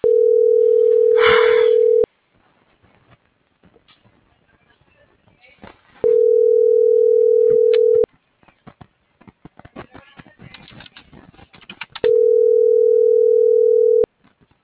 Ringback tone (North America), 440 + 480 Hz.
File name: CENTRAL MAYAB / out michelle.martinez